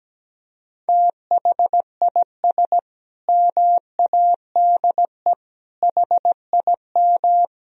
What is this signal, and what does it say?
Morse code, heard as a keyed tone, about 17 wpm, 700 Hz. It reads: THIS MADE HIM
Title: Sentences from Top 500 Words and Abbreviations 17wpm